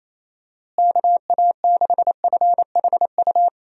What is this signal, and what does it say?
Morse code, heard as a keyed tone, about 27 wpm, 700 Hz. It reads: KA6FHU
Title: USA Call Signs - 6 Characters - Repeated 3x - 28wpm